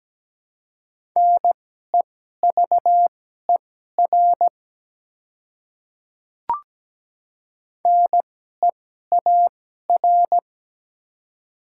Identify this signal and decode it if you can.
Morse code, heard as a keyed tone, about 17 wpm, spaced as for 12 wpm, 700 Hz. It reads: NEVER NEAR
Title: 2x Character Spacing - Top 300 Words - 17wpm